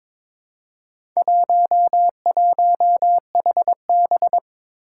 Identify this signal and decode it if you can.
Morse code, heard as a keyed tone, about 23 wpm, 700 Hz. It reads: 11HB
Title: Four Challenging Letter-Number Combinations 22wpm